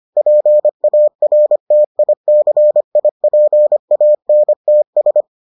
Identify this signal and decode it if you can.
Morse code, heard as a keyed tone, about 25 wpm, 600 Hz. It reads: PARTICIPANTS